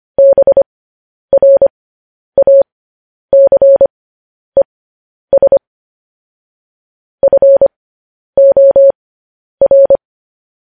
Morse code, heard as a keyed tone, about 25 wpm, spaced as for 10 wpm, 550 Hz. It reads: BRACES FOR
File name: podcast / News headlines-1x-25x10